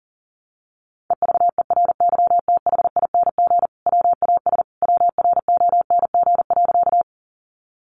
Morse code, heard as a keyed tone, about 40 wpm, 700 Hz. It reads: EVERYTHING WAS WRONG.
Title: Sentences from Top 900 Words 40wpm